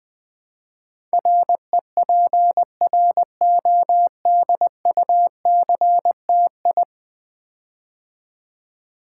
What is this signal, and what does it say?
Morse code, heard as a keyed tone, about 20 wpm, 700 Hz. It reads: REPRODUCTI
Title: Build A Word - 5 Letters At A Time - 12-Letter Words - 20wpm